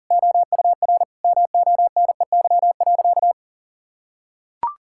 Morse code, heard as a keyed tone, about 40 wpm, 700 Hz. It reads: OUR MONEY.
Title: Sentences from Top 1600 Words 40wpm